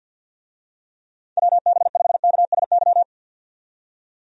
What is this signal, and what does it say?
Morse code, heard as a keyed tone, about 52 wpm, 700 Hz. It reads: WB5KSY
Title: USA Call Signs - 6 Characters - 50wpm